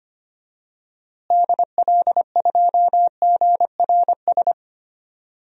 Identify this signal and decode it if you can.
Morse code, heard as a keyed tone, about 25 wpm, 700 Hz. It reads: DL2GRS
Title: DXCC-100 Call Signs - 6 Characters - Repeated 3x - 25wpm